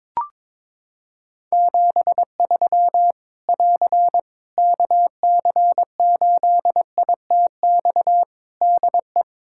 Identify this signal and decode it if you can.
Morse code, heard as a keyed tone, about 22 wpm, 700 Hz. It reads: 73 <AR> KC8ITX DE
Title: Practice QSO - Exchange 4 - Ending - 22wpm